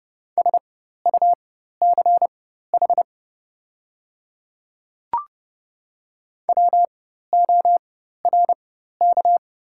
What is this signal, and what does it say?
Morse code, heard as a keyed tone, about 29 wpm, spaced as for 14 wpm, 700 Hz. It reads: SUCH WORK
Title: 4x Character Spacing - Top 200 Words - 30wpm